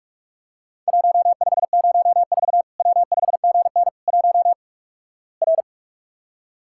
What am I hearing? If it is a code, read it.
Morse code, heard as a keyed tone, about 46 wpm, 700 Hz. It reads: 1504 W5GN 1 R
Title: Worked All Europe DX Contest - QTC Exchange - 45wpm